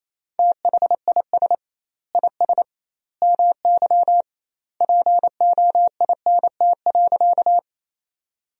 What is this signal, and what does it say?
Morse code, heard as a keyed tone, about 28 wpm, 700 Hz. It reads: THIS IS MY POINT.
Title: 2x Word-Spacing - Sentences from Top 200 Words - 28wpm